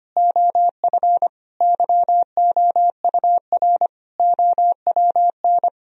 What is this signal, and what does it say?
Morse code, heard as a keyed tone, about 25 wpm, 700 Hz. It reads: OF YOUR OWN
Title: Sentences from Top 1000 Words - Review Part 4 - 25wpm